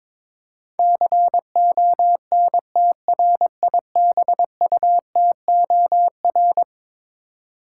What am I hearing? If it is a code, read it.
Morse code, heard as a keyed tone, about 22 wpm, 700 Hz. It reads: CONTRIBUTOR
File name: practice / 22wpm / Eleven Letter Words - 22wpm